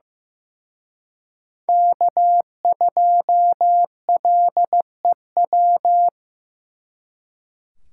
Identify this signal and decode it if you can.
Morse code, heard as a keyed tone, about 18 wpm, 700 Hz. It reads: K2LEW
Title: Speed Racing - USA Call Signs 15wpm